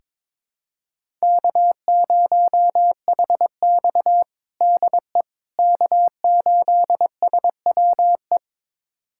Morse code, heard as a keyed tone, about 22 wpm, 700 Hz. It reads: K0HX DE K8SWE
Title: USA Callsign DE Callsign 22wpm